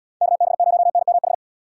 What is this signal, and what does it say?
Morse code, heard as a keyed tone, about 75 wpm, 700 Hz. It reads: KB0NMH